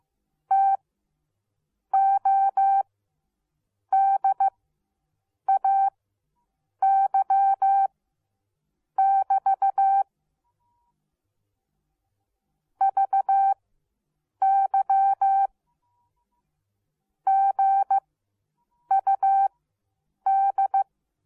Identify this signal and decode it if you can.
Morse code, heard as a keyed tone, about 15 wpm, spaced as for 7 wpm, 780 Hz. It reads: TODAY <BT> VY GUD